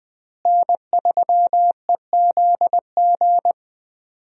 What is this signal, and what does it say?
Morse code, heard as a keyed tone, about 20 wpm, 700 Hz. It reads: N3EZG
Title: USA Call Signs - 5 Characters - Repeated 3x - 20wpm